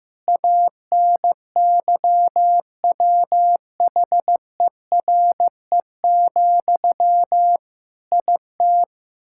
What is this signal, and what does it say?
Morse code, heard as a keyed tone, about 16 wpm, 700 Hz. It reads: ANYWHERE, IT
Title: Sentences from Top 1600 Words 15wpm